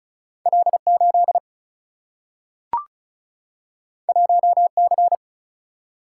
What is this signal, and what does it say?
Morse code, heard as a keyed tone, about 36 wpm, 700 Hz. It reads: L8 1C
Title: Single-Letter Single-Number 35wpm